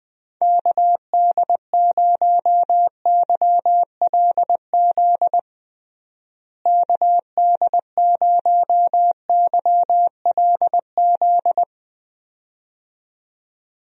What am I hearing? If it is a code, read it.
Morse code, heard as a keyed tone, about 20 wpm, 700 Hz. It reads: KD0YLZ KD0YLZ
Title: USA Call Signs - 6 Characters - Repeated 2x - 20wpm